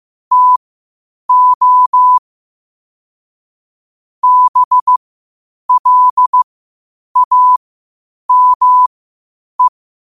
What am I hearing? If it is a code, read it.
Morse code, heard as a keyed tone, about 15 wpm, spaced as for 8 wpm, 1000 Hz. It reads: TO BLAME
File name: qotdfiles / 8wpm